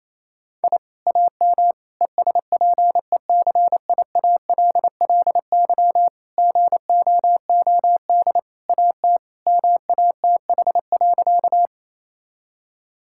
Morse code, heard as a keyed tone, about 28 wpm, 700 Hz. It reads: I AM ESPECIALLY GOOD AT MATH.
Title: Sentences from Top 1000 Words - Review Part 6 - 28wpm